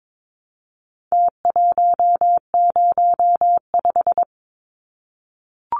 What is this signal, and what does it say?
Morse code, heard as a keyed tone, about 22 wpm, 700 Hz. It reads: T105
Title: Numbers - Contest Style - 4-Digit 22wpm